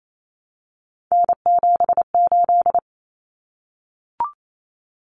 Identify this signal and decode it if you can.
Morse code, heard as a keyed tone, about 28 wpm, 700 Hz. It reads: N78
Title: Common Cut Numbers - 3-Digit 28wpm